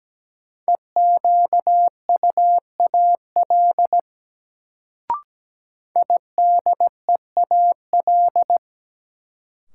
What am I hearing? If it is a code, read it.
Morse code, heard as a keyed tone, about 17 wpm, 700 Hz. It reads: EQUAL IDEAL
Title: Sentences from Top 1400 Words - New Word Drill - 17wpm